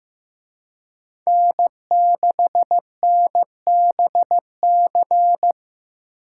Morse code, heard as a keyed tone, about 18 wpm, 700 Hz. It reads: N6NBC
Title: Speed Racing - USA Call Signs 15wpm